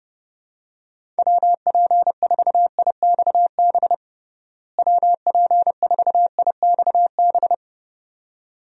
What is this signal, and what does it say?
Morse code, heard as a keyed tone, about 30 wpm, 700 Hz. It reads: WP4IXB WP4IXB
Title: USA Call Signs - 6 Characters - Repeated 2x - 30wpm